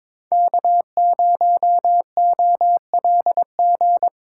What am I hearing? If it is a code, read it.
Morse code, heard as a keyed tone, about 25 wpm, 700 Hz. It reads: K0OLG
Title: Speed Racing - USA 1x3 Call Signs 22wpm